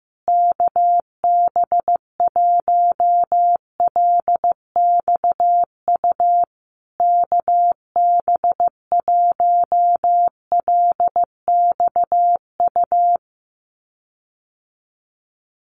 Morse code, heard as a keyed tone, about 15 wpm, 700 Hz. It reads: KB1LXU KB1LXU
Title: USA Call Signs Repeated 3x 15wpm